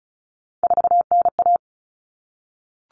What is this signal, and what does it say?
Morse code, heard as a keyed tone, about 34 wpm, 700 Hz. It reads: 4NA
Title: Common Cut Numbers - 3-Digit 35wpm